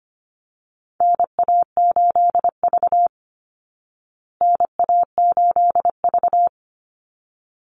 Morse code, heard as a keyed tone, about 25 wpm, 700 Hz. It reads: NA8V NA8V
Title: Call Signs Repeated 3x - 25wpm